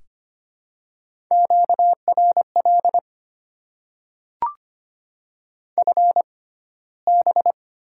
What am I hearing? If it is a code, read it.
Morse code, heard as a keyed tone, about 25 wpm, spaced as for 15 wpm, 700 Hz. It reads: QRL FB
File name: Top 20 QSO Elements - 6x Character Spacing - 25wpm